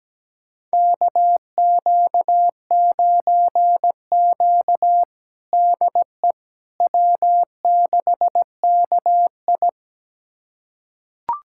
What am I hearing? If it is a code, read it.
Morse code, heard as a keyed tone, about 17 wpm, 700 Hz. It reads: KQ9Q DE W6KI
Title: USA Callsign DE Callsign - 4 Characters - 17wpm